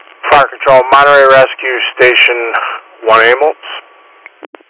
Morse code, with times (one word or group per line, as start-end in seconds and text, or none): none